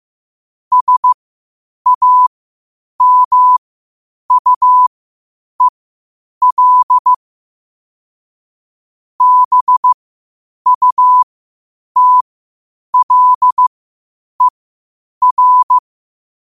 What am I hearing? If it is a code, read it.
Morse code, heard as a keyed tone, about 15 wpm, spaced as for 8 wpm, 1000 Hz. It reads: SAMUEL BUTLER
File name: qotdfiles / 8wpm